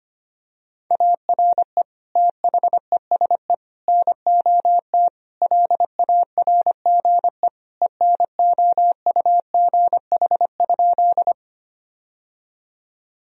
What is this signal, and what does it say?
Morse code, heard as a keyed tone, about 25 wpm, 700 Hz. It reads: ARE THESE NOT LARGE ENOUGH?
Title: Sentences from Top 1000 Words - Review Part 1 - 25wpm